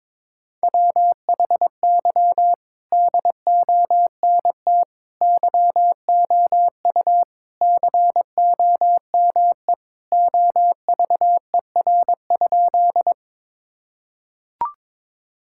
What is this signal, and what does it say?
Morse code, heard as a keyed tone, about 22 wpm, 700 Hz. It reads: WHY DONT YOU COME OVER?
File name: practice / 22wpm / Sentences with 4-Word Phrases - 22wpm